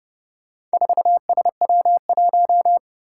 Morse code, heard as a keyed tone, about 30 wpm, 700 Hz. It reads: 4SW1
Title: Four Challenging Letter-Number Combinations 30wpm